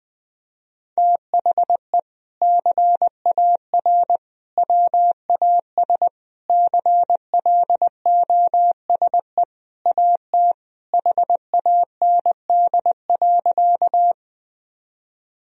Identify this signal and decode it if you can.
Morse code, heard as a keyed tone, about 20 wpm, 700 Hz. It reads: THE CAR WAS CLOSE AT HAND.
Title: Sentences from Top 1000 Words - Review Part 1 - 20wpm